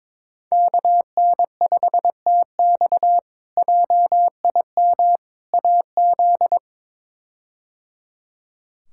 Morse code, heard as a keyed tone, about 22 wpm, 700 Hz. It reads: KN5TX JIM AZ
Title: K1USN SST QSO Practice - 22wpm